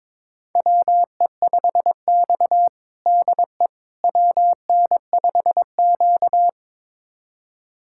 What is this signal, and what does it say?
Morse code, heard as a keyed tone, about 22 wpm, 700 Hz. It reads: WE5X DE WN5Q
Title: USA Callsign DE Callsign - 4 Characters - 22wpm